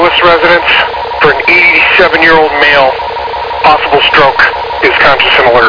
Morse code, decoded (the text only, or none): none